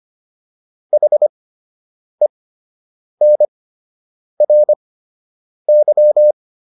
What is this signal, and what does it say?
Morse code, heard as a keyed tone, about 25 wpm, spaced as for 8 wpm, 600 Hz. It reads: HENRY